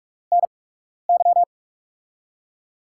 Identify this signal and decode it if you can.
Morse code, heard as a keyed tone, about 44 wpm, spaced as for 12 wpm, 700 Hz. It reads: NY
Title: USA Call Signs - 5 Characters - 8x Character Spacing - 45wpm